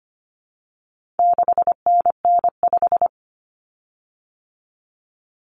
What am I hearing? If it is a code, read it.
Morse code, heard as a keyed tone, about 25 wpm, 700 Hz. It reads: 6NN5
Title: Common Cut Numbers - 4-Digit 25wpm